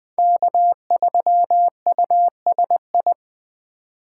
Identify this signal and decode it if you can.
Morse code, heard as a keyed tone, about 21 wpm, 700 Hz. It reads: K3USI